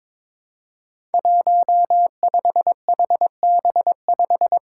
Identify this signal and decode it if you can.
Morse code, heard as a keyed tone, about 22 wpm, 700 Hz. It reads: 15HB5
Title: Five Challenging Letter-Number Combinations 22wpm